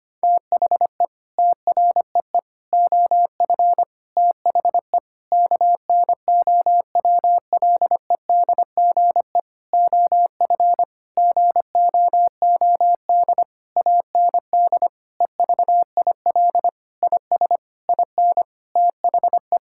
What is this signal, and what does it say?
Morse code, heard as a keyed tone, about 25 wpm, 700 Hz. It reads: THE TREE OF THE KNOWLEDGE OF GOOD AND EVIL IS IN THE
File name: practice / 25wpm / Sentences with 9-Word Phrases - 25wpm